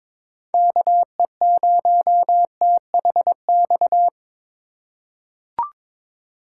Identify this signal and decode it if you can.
Morse code, heard as a keyed tone, about 22 wpm, 700 Hz. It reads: KE0THX
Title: USA Call Signs - 6 Characters - 2x Character Spacing - 22wpm